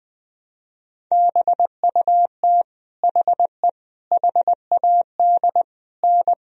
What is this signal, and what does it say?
Morse code, heard as a keyed tone, about 20 wpm, 700 Hz. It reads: BUT HE HAD N
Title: Sentences from Top 100 Words and Abbreviations 20wpm